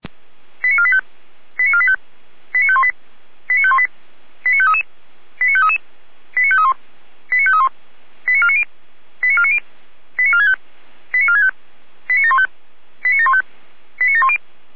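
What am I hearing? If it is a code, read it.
Five-tone selective call (ZVEI), heard as quick runs of five tones, tones of about 70 ms. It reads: 89485 89485 89518 89518 89533 89533 89531 89531 89490 89490 89465 89465 89715 89715 89710
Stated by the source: Five-tone ZVEI selective call received on an SDR, from the feuerwehr-orlamuende.org fire brigade server.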